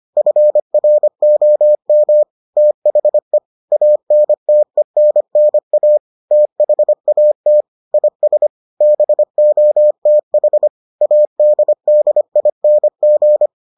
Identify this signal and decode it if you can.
Morse code, heard as a keyed tone, about 25 wpm, 600 Hz. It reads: FROM THE ANTENNA THAT IS BOTH ADDING